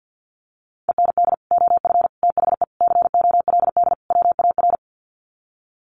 Morse code, heard as a keyed tone, about 49 wpm, 700 Hz. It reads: END OF THE COLD WAR
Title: Sets of 5 Words Encore 50wpm